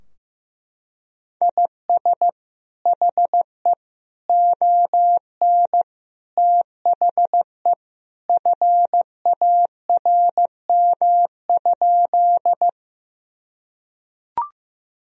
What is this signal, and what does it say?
Morse code, heard as a keyed tone, about 15 wpm, 700 Hz. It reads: IS HE ON THE FARM?
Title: Sentences from Top 1000 Words - Review Part 2 - 15wpm